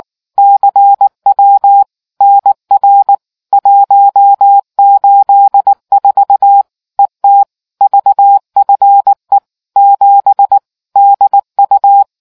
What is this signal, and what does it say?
Morse code, heard as a keyed tone, about 19 wpm, 800 Hz. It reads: CW NR 184 ET VFE 7 DU